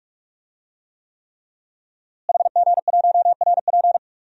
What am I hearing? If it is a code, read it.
Morse code, heard as a keyed tone, about 46 wpm, 700 Hz. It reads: HG1RP
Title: DXCC-100 Call Signs - 5 Characters - Repeated 3x - 45wpm